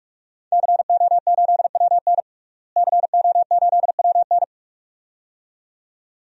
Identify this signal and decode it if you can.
Morse code, heard as a keyed tone, about 46 wpm, 700 Hz. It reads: CO8WN CO8WN
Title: DXCC-100 Call Signs - 5 Characters - Repeated 2x - 45wpm